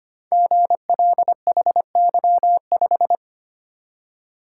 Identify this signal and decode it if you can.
Morse code, heard as a keyed tone, about 25 wpm, 700 Hz. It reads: GLHY5
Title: Code Groups - 5 Letter:Numbers - 25wpm